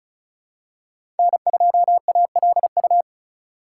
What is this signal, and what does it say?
Morse code, heard as a keyed tone, about 36 wpm, 700 Hz. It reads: N2ALU
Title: USA Call Signs - 5 Characters - 3x Character Spacing - 35wpm